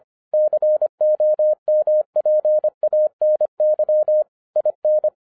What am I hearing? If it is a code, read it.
Morse code, heard as a keyed tone, about 25 wpm, 610 Hz. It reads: COMPANY IN